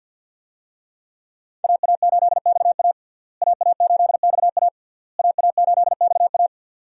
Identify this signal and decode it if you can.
Morse code, heard as a keyed tone, about 45 wpm, 700 Hz. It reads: AA8XA AA8XA AA8XA
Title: USA Call Signs - 5 Characters - Repeated 3x - 50wpm